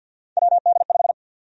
Morse code, heard as a keyed tone, about 51 wpm, 700 Hz. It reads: WD5
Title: USA Call Sign Prefixes - 50wpm